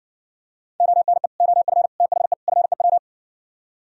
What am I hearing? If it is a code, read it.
Morse code, heard as a keyed tone, about 60 wpm, 700 Hz. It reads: ONE OF THE FEW